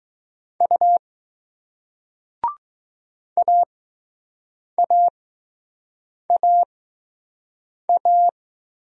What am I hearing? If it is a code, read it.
Morse code, heard as a keyed tone, about 19 wpm, spaced as for 6 wpm, 700 Hz. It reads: U AAAA